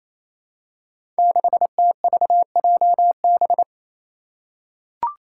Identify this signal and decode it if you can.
Morse code, heard as a keyed tone, about 27 wpm, 700 Hz. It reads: 6TVJB
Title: Five Challenging Letter-Number Combinations 28wpm